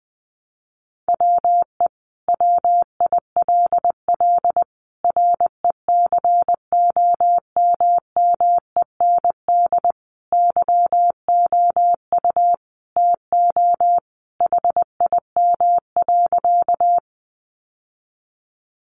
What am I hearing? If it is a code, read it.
Morse code, heard as a keyed tone, about 19 wpm, 700 Hz. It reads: WE WILL RECOMMEND YOU TO HIM.
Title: Sentences from Top 1200 Words 20wpm